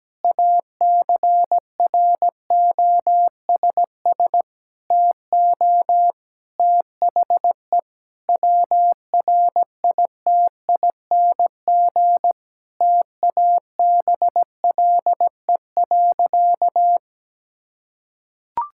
Morse code, heard as a keyed tone, about 17 wpm, 700 Hz. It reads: ACROSS TO THE WRITING TABLE.
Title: Sentences from Top 1000 Words - Review Part 5 - 17wpm